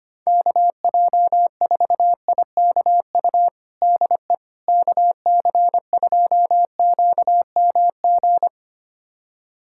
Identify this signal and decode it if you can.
Morse code, heard as a keyed tone, about 25 wpm, 700 Hz. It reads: KJ4IKU DE KC2QMG
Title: USA Callsign DE Callsign - 6 Characters - 25wpm